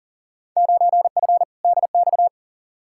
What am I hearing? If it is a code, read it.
Morse code, heard as a keyed tone, about 40 wpm, 700 Hz. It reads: 9F DX
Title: Field Day Exchanges Encore - 40wpm